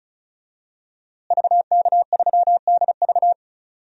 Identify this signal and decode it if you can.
Morse code, heard as a keyed tone, about 36 wpm, 700 Hz. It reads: VK3DV
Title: DXCC-100 Call Signs - 5 Characters - Repeated 2x - 35wpm